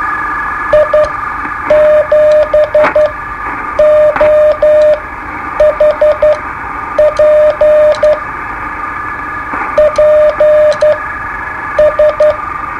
Morse code, heard as a keyed tone, about 11 wpm, spaced as for 8 wpm, 610 Hz, with radio noise. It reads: I7OHP PS
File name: qso-01